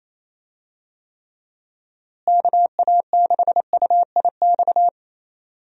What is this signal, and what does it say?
Morse code, heard as a keyed tone, about 28 wpm, 700 Hz. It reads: KA6UIX
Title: USA Call Signs - 6 Characters - Repeated 2x - 28wpm